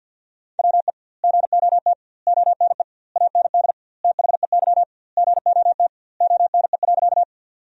Morse code, heard as a keyed tone, about 50 wpm, 700 Hz. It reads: WE GOT ONE AND THEY GOT ONE.
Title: 2x Word-Spacing - Sentences from Top 300 Words - 50wpm